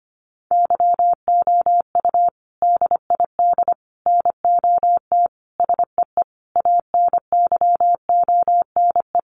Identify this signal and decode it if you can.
Morse code, heard as a keyed tone, about 25 wpm, 700 Hz. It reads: YOU DID NOT SEE ANYONE
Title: Sentences from Top 900 Words 25wpm